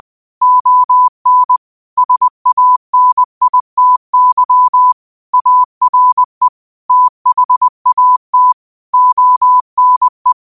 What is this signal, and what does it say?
Morse code, heard as a keyed tone, about 20 wpm, 1000 Hz. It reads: ON SANITY ARE THAT ONE